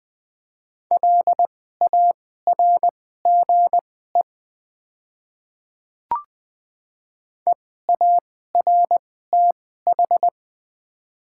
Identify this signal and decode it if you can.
Morse code, heard as a keyed tone, about 20 wpm, spaced as for 14 wpm, 700 Hz. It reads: LARGE EARTH
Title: 2x Character Spacing - Top 300 Words - 20wpm